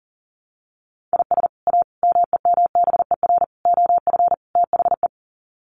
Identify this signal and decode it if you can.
Morse code, heard as a keyed tone, about 40 wpm, 700 Hz. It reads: IS A MEMBER OF THE
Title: Sets of 5 Words 40wpm